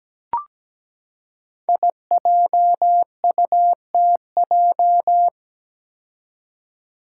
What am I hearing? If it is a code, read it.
Morse code, heard as a keyed tone, about 17 wpm, 700 Hz. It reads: IJUTJ
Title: Five Challenging Letter-Number Combinations 17wpm